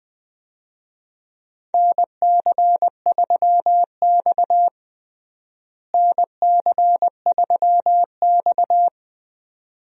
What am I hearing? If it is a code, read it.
Morse code, heard as a keyed tone, about 20 wpm, 700 Hz. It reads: NC3X NC3X
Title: USA Call Signs - 4 Characters - Repeated 2x - 20wpm